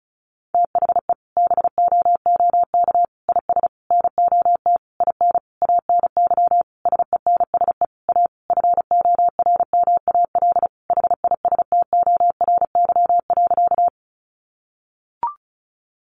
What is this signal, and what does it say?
Morse code, heard as a keyed tone, about 34 wpm, 700 Hz. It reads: THE BOOK IS NOT IN ANY SENSE A FORMAL HISTORY.
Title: Sentences from Top 1300 Words 35wpm